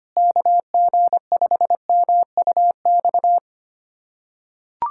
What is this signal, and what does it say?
Morse code, heard as a keyed tone, about 25 wpm, 700 Hz. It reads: KG5MUX
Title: USA Callsign DE Callsign - 6 Characters - 25wpm